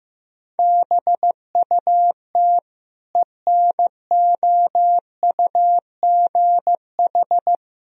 Morse code, heard as a keyed tone, about 15 wpm, 700 Hz. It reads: BUT ENOUGH
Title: Sentences from Top 1000 Words - Review Part 2 - 15wpm